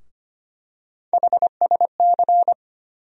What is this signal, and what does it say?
Morse code, heard as a keyed tone, about 25 wpm, 700 Hz. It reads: HSC
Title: USA Call Sign Postfixes - 3 Letters 25wpm